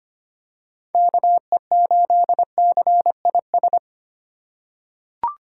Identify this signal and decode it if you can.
Morse code, heard as a keyed tone, about 25 wpm, 700 Hz. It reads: KE8CIS
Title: USA Call Signs - 6 Characters - 3x Character Spacing - 25wpm